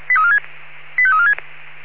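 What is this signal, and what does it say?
Five-tone selective call (ZVEI), heard as quick runs of five tones, tones of about 70 ms. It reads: 85346
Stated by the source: ZVEI five-tone selective call received by SDR.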